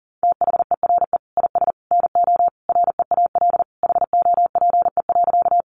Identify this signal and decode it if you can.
Morse code, heard as a keyed tone, about 40 wpm, 700 Hz. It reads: THERE IS NO REAL HOPE.
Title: Sentences from Top 1000 Words 40wpm